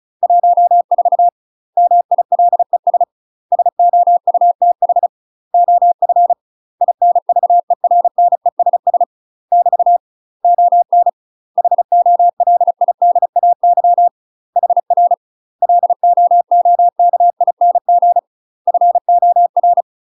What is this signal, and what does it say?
Morse code, heard as a keyed tone, about 36 wpm, 700 Hz. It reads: 14 MILES SOUTH OF INVERNESS <BT> ON HOLIDAY HR LOOKING FOR
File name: UK MASTER QSOs TRAINING / 35WPM / UK MASTER QSOs 0029